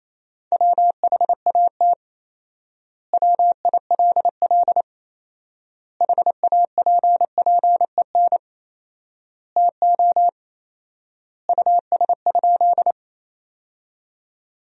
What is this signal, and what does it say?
Morse code, heard as a keyed tone, about 28 wpm, 700 Hz. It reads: WHAT WILL HAPPEN TO US?
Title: 4x Word-Spacing - Sentences from Top 400 Words - 28wpm